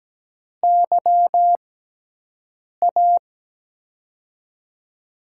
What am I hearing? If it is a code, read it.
Morse code, heard as a keyed tone, about 17 wpm, spaced as for 6 wpm, 700 Hz. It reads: YA
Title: USA Call Signs - 6 Characters - 6x Character Spacing - 17wpm